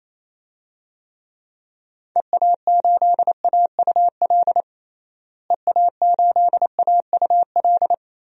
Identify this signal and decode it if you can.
Morse code, heard as a keyed tone, about 28 wpm, 700 Hz. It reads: EA8AUL EA8AUL
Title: DXCC-100 Call Signs - 6 Characters - Repeated 3x - 28wpm